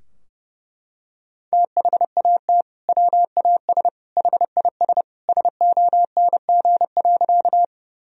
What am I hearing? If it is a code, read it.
Morse code, heard as a keyed tone, about 30 wpm, 700 Hz. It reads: THAT WAS HIS SONG.
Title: Sentences from Top 1000 Words - Review Part 2 - 30wpm